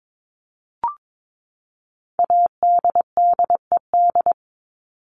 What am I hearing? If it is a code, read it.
Morse code, heard as a keyed tone, about 22 wpm, 700 Hz. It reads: ADDED
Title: ED Suffix - Top 100 Words with Base Words 22wpm